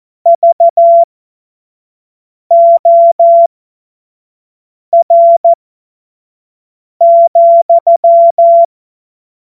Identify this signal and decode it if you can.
Morse code, heard as a keyed tone, about 14 wpm, spaced as for 5 wpm, 680 Hz. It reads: VOR,
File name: Witz-05-August-2025-25-F70